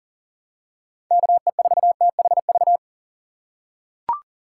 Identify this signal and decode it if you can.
Morse code, heard as a keyed tone, about 40 wpm, 700 Hz. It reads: KE4THV
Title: USA Call Signs - 6 Characters - 2x Character Spacing - 40wpm